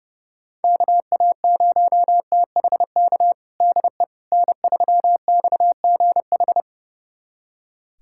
Morse code, heard as a keyed tone, about 30 wpm, 690 Hz. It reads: KA0THK DE N3XGH
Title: USA Callsign DE Callsign 30wpm